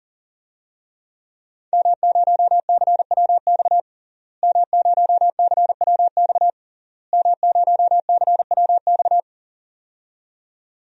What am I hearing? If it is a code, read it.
Morse code, heard as a keyed tone, about 41 wpm, 700 Hz. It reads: M0CWX M0CWX M0CWX
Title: DXCC-100 Call Signs - 5 Characters - Repeated 3x - 40wpm